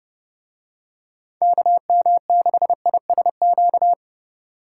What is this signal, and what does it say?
Morse code, heard as a keyed tone, about 29 wpm, 700 Hz. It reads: KM6ISQ